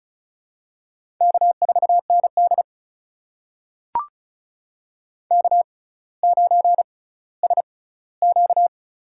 Morse code, heard as a keyed tone, about 34 wpm, spaced as for 18 wpm, 700 Hz. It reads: K4ND K9SQ
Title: USA Call Signs - 4 Characters - 6x Character Spacing - 35wpm